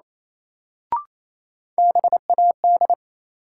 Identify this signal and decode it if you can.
Morse code, heard as a keyed tone, about 28 wpm, 700 Hz. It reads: BAD
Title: Sentences from Top 700 Words - New Word Drill - 28wpm